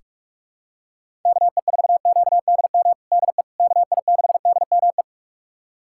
Morse code, heard as a keyed tone, about 46 wpm, 700 Hz. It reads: KE4QDM DE KI6DME
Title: USA Callsign DE Callsign - 6 Characters - 45wpm